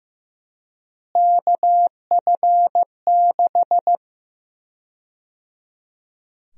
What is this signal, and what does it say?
Morse code, heard as a keyed tone, about 15 wpm, 700 Hz. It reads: KF6
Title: USA Call Sign Prefixes - Repeated 2x - 15wpm